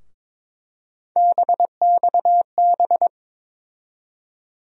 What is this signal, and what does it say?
Morse code, heard as a keyed tone, about 22 wpm, 700 Hz. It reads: BXB
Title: USA Call Sign Postfixes - 3 Letters 22wpm